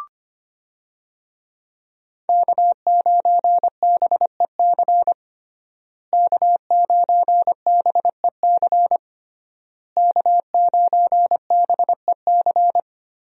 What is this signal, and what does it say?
Morse code, heard as a keyed tone, about 25 wpm, 700 Hz. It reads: K9BEC K9BEC K9BEC
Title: USA Call Signs - 5 Characters - Repeated 3x - 25wpm